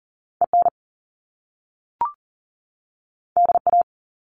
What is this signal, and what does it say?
Morse code, heard as a keyed tone, about 40 wpm, 700 Hz. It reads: EN DA
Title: Top Two Letter Combinations in Words 40wpm